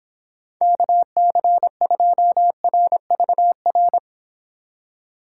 Morse code, heard as a keyed tone, about 26 wpm, 700 Hz. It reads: KC2RVR